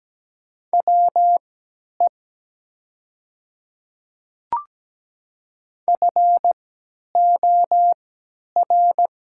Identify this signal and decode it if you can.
Morse code, heard as a keyed tone, about 17 wpm, spaced as for 10 wpm, 700 Hz. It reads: WE FOR
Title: 3x Character Spacing - Top 200 Words - 17wpm